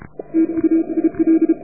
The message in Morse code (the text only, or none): TUIL